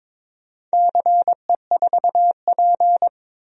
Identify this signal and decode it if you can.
Morse code, heard as a keyed tone, about 21 wpm, 700 Hz. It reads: CE4P
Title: DXCC-100 Call Signs - 4 Characters - 22wpm